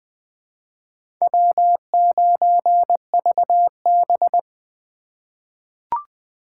Morse code, heard as a keyed tone, about 20 wpm, 700 Hz. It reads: W9VB
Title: USA Call Signs - 4x Character Spacing - 20wpm